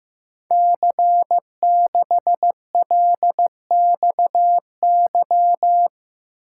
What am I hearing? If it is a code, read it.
Morse code, heard as a keyed tone, about 15 wpm, 700 Hz. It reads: C6LXY
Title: Code Groups - 5 Letter:Numbers - 15wpm